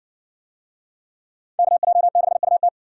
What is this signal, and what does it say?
Morse code, heard as a keyed tone, about 60 wpm, 700 Hz. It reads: KJ6UT